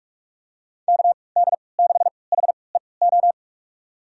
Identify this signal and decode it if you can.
Morse code, heard as a keyed tone, about 46 wpm, spaced as for 26 wpm, 700 Hz. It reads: KD6HEO
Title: USA Call Signs - 3x Character Spacing - 45wpm